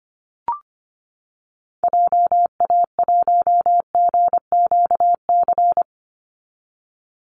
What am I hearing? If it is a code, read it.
Morse code, heard as a keyed tone, about 25 wpm, 700 Hz. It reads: JA1GQC
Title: Call Signs - 25wpm